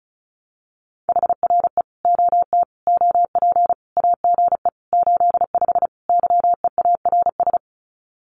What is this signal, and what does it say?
Morse code, heard as a keyed tone, about 34 wpm, 700 Hz. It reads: HRE OT OP AGE 85 YEARS
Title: Common QSO Phrases 35wpm